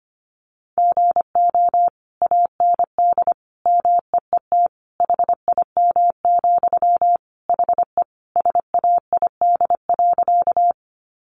Morse code, heard as a keyed tone, about 25 wpm, 700 Hz. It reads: GO AND MEET HIM, HE SAID.
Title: Sentences from Top 1100 Words 25wpm